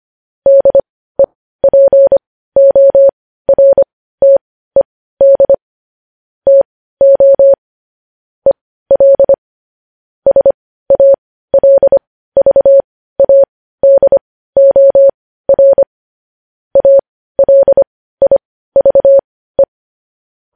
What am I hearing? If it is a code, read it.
Morse code, heard as a keyed tone, about 25 wpm, spaced as for 15 wpm, 550 Hz. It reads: DEPORTED TO EL SALVADOR ALIVE
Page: news-headlines-in-morse-code-at-15-wpm